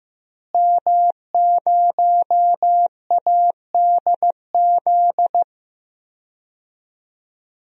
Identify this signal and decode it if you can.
Morse code, heard as a keyed tone, about 15 wpm, 700 Hz. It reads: M0ADZ
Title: DXCC-100 Call Signs - 5 Characters - Repeated 3x - 15wpm